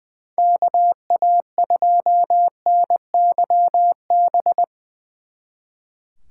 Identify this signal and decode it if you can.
Morse code, heard as a keyed tone, about 22 wpm, 700 Hz. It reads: KA2NYB
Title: Speed Racing - USA 2x3 Call Signs 20wpm